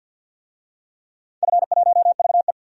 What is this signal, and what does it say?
Morse code, heard as a keyed tone, about 47 wpm, 700 Hz. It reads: F1VE
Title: DXCC-100 Call Signs - 4 Characters - 50wpm